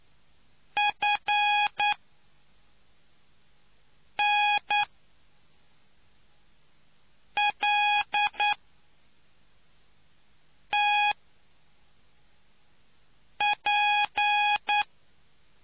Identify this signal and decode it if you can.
Morse code, heard as a keyed tone, about 9 wpm, spaced as for 3 wpm, 830 Hz. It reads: FNLTP